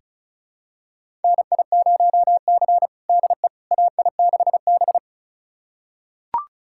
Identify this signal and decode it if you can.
Morse code, heard as a keyed tone, about 34 wpm, 700 Hz. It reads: NI0C DE AI6B
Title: USA Callsign DE Callsign - 4 Characters - 35wpm